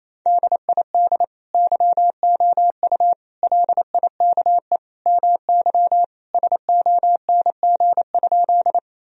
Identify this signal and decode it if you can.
Morse code, heard as a keyed tone, about 27 wpm, 700 Hz. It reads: DID YOU LIKE MY SONG?
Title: Sentences from Top 1000 Words - Review Part 2 - 28wpm